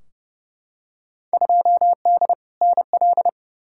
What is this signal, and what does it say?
Morse code, heard as a keyed tone, about 30 wpm, 700 Hz. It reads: 2D NL
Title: Field Day Exchanges Encore - 30wpm